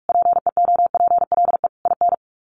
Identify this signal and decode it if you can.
Morse code, heard as a keyed tone, about 44 wpm, 700 Hz. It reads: PEOPLE IN
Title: Sets of 5 Words 45wpm